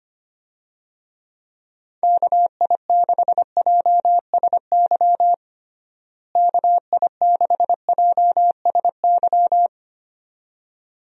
Morse code, heard as a keyed tone, about 25 wpm, 700 Hz. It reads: KI6JSY KI6JSY
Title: USA Call Signs - 6 Characters - Repeated 2x - 25wpm